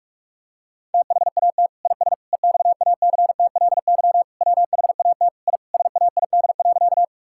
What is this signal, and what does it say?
Morse code, heard as a keyed tone, about 46 wpm, 700 Hz. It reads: THAT IS EXACTLY WHAT I SAID.
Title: Sentences from Top 1000 Words - Review Part 6 - 45wpm